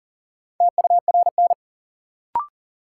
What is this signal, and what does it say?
Morse code, heard as a keyed tone, about 40 wpm, 700 Hz. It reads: TURN
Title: Sentences from Top 200 Words - New Word Drill - 40wpm